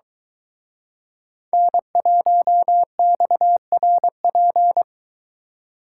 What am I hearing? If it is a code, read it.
Morse code, heard as a keyed tone, about 24 wpm, 700 Hz. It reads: N1XRP